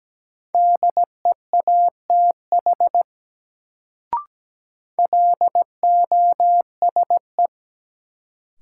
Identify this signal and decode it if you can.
Morse code, heard as a keyed tone, about 17 wpm, 700 Hz. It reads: DEATH LOSE
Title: Sentences from Top 1100 Words - New Word Drill - 17wpm